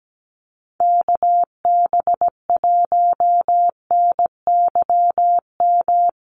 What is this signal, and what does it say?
Morse code, heard as a keyed tone, about 17 wpm, 700 Hz. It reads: KB1NYM
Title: USA Call Signs Repeated 2x 17wpm